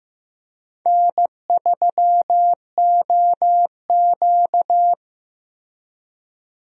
Morse code, heard as a keyed tone, about 15 wpm, 700 Hz. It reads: N3OQ
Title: USA Call Signs - 4 Characters - Repeated 2x - 15wpm